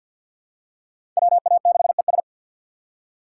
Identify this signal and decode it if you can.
Morse code, heard as a keyed tone, about 47 wpm, 700 Hz. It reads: WA6ES
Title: USA Call Signs - 6x Character Spacing - 50wpm